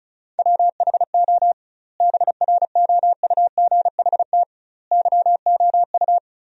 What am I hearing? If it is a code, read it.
Morse code, heard as a keyed tone, about 36 wpm, 700 Hz. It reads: WHO BROUGHT YOU
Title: 2x Word-Spacing - Sentences from Top 500 Words - 35wpm